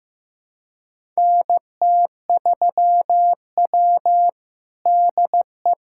Morse code, heard as a keyed tone, about 15 wpm, 700 Hz. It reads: NT3W DE
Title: USA Callsign DE Callsign - 4 Characters - 15wpm